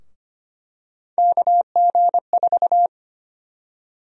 Morse code, heard as a keyed tone, about 25 wpm, 700 Hz. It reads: KG4
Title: USA Call Sign Prefixes - Repeated 2x - 25wpm